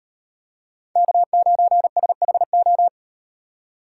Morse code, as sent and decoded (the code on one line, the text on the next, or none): -.- ----. ... .... ---
K9SHO